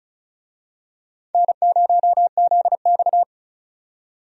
Morse code, heard as a keyed tone, about 34 wpm, 700 Hz. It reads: N0ZX
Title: USA Call Signs - 4 Characters - 35wpm